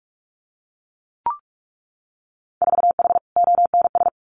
Morse code, heard as a keyed tone, about 46 wpm, 690 Hz. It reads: 4H ONS
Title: Winter Field Day Exchanges Encore 45wpm